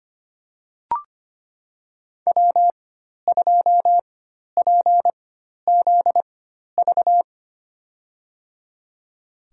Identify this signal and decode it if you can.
Morse code, heard as a keyed tone, about 25 wpm, spaced as for 12 wpm, 700 Hz. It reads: W2PZV
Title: USA Call Signs - 5 Characters - 4x Character Spacing - 25wpm